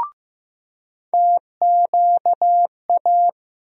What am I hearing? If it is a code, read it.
Morse code, heard as a keyed tone, about 15 wpm, 700 Hz. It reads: TQA
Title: USA Call Sign Postfixes - 3 Letters 15wpm